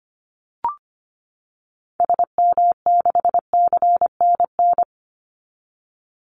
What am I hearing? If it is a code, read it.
Morse code, heard as a keyed tone, about 25 wpm, 700 Hz. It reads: SM6CNN
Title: Call Signs - 25wpm